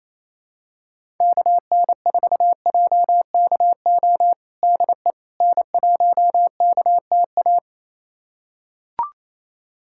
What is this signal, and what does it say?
Morse code, heard as a keyed tone, about 28 wpm, 700 Hz. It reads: KN4JKO DE N1KTA
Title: USA Callsign DE Callsign 28wpm